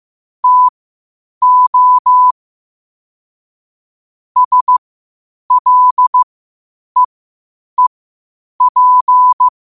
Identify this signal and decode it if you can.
Morse code, heard as a keyed tone, about 15 wpm, spaced as for 8 wpm, 1000 Hz. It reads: TO SLEEP
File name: qotdfiles / 8wpm